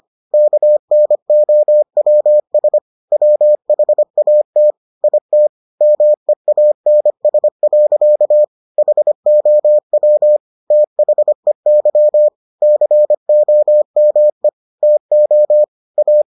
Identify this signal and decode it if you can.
Morse code, heard as a keyed tone, about 25 wpm, 600 Hz. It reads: KNOWS WHAT IT MEANS. HOW THEY COME TO A